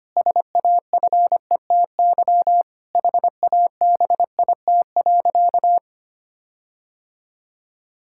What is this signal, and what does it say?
Morse code, heard as a keyed tone, about 25 wpm, 700 Hz. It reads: SAFETY HABIT.
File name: practice / 25wpm / Difficult Sentences with Hints - 25wpm